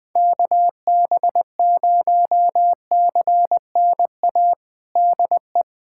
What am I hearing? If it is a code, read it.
Morse code, heard as a keyed tone, about 20 wpm, 700 Hz. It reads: KB0CNA DE
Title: USA Callsign DE Callsign - 6 Characters - 20wpm